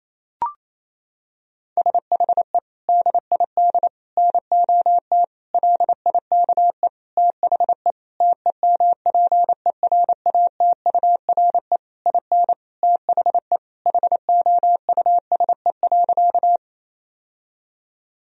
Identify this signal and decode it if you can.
Morse code, heard as a keyed tone, about 28 wpm, 700 Hz. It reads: SHE DID NOT LIKE THE TEMPERATURE IN THE HOUSE.
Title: Sentences from Top 1000 Words - Review Part 4 - 28wpm